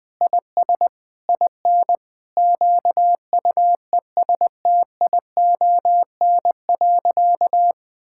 Morse code, heard as a keyed tone, about 20 wpm, 700 Hz. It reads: IS IN QUESTION.
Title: Sentences from Top 1000 Words - Review Part 6 - 20wpm